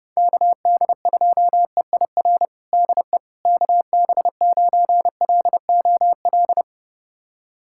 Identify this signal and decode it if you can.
Morse code, heard as a keyed tone, about 30 wpm, 700 Hz. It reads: KD2EIR DE KB9LOL
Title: USA Callsign DE Callsign - 6 Characters - 30wpm